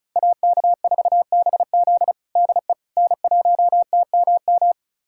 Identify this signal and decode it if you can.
Morse code, heard as a keyed tone, about 36 wpm, 700 Hz. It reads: AK4BZ DE N1TMM
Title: USA Callsign DE Callsign - 5 Characters - 35wpm